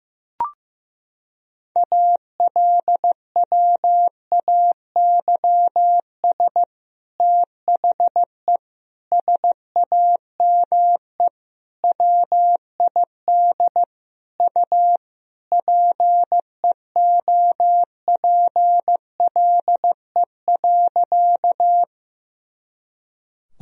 Morse code, heard as a keyed tone, about 16 wpm, 700 Hz. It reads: ALWAYS THE SAME WID U PEOPLE.
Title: Sentences from Top 300 Words and Abbreviations 15wpm